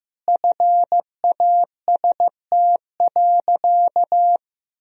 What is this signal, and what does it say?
Morse code, heard as a keyed tone, about 15 wpm, 700 Hz. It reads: FAST.
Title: Sentences from Top 1000 Words - Review Part 2 - 15wpm